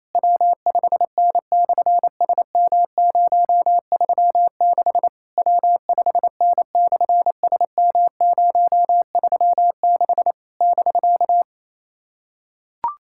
Morse code, heard as a keyed tone, about 27 wpm, 700 Hz. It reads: W5N/SM036 W5N/SM036 <BK>
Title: SOTA QSO Practice 28wpm